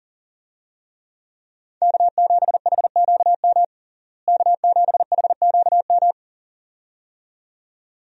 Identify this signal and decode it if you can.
Morse code, heard as a keyed tone, about 40 wpm, 700 Hz. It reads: K7HQM K7HQM
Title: USA Call Signs - 5 Characters - Repeated 2x - 40wpm